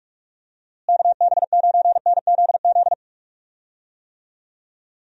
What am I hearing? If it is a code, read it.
Morse code, heard as a keyed tone, about 46 wpm, 700 Hz. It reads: KB9NZZ
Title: USA Call Signs - 6 Characters - Repeated 2x - 45wpm